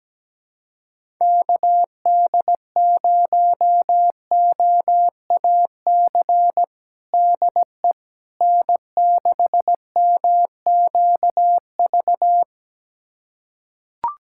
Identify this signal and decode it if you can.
Morse code, heard as a keyed tone, about 17 wpm, 700 Hz. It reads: KD0OAC DE N6MQV
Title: USA Callsign DE Callsign 17wpm